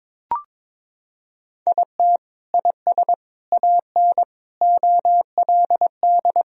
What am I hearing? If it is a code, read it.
Morse code, heard as a keyed tone, about 22 wpm, 700 Hz. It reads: IT IS AN OLD
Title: Sentences from Top 1000 Words - Review Part 4 - 22wpm